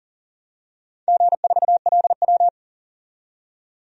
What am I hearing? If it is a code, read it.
Morse code, heard as a keyed tone, about 40 wpm, 700 Hz. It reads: G4LW
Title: DXCC-100 Call Signs - 4 Characters - 40wpm